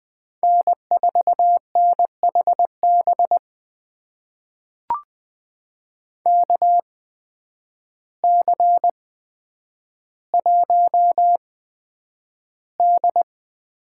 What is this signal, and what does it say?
Morse code, heard as a keyed tone, about 20 wpm, spaced as for 9 wpm, 700 Hz. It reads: N4NHB KC1D
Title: USA Call Signs - 8x Character Spacing - 20wpm